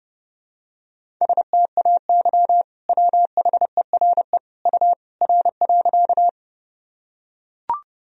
Morse code, heard as a keyed tone, about 30 wpm, 700 Hz. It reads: STAY WHERE U R.
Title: Sentences from Top 500 Words and Abbreviations 30wpm